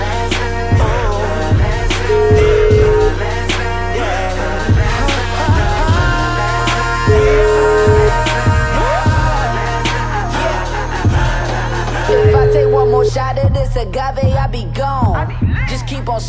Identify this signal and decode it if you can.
Ringback tone (Europe, 425 Hz), with noise.